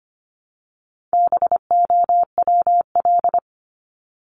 Morse code, heard as a keyed tone, about 25 wpm, 700 Hz. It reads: BOWL
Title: Top 1001-1250 Words 25wpm